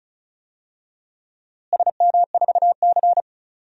Morse code, heard as a keyed tone, about 36 wpm, 700 Hz. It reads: SM4C
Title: DXCC-100 Call Signs - 4 Characters - Repeated 2x - 35wpm